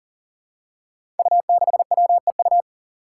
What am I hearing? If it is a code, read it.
Morse code, heard as a keyed tone, about 37 wpm, 700 Hz. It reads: U6WEU
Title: Five Challenging Letter-Number Combinations 40wpm